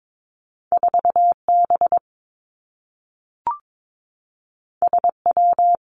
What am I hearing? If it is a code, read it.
Morse code, heard as a keyed tone, about 22 wpm, 700 Hz. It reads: 4 B S W
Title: Two Challenging Letter-Number Combinations 22wpm